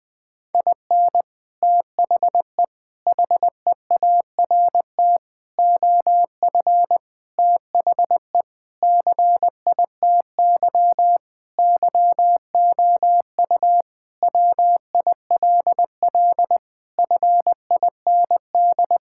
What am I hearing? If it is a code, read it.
Morse code, heard as a keyed tone, about 20 wpm, 700 Hz. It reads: IN THE HEART OF THE CITY YOU WILL FIND
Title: Sentences with 3-Word Phrases - Spoken Beginning - 20wpm